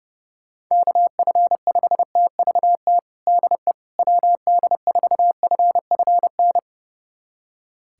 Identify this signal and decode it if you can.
Morse code, heard as a keyed tone, about 30 wpm, 700 Hz. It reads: KF5TVT DE WD4FFN
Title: USA Callsign DE Callsign - 6 Characters - 30wpm